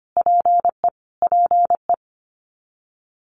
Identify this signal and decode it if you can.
Morse code, heard as a keyed tone, about 25 wpm, 700 Hz. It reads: PE PE
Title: Canadian Provinces and Territories - Repeated 3x - 25wpm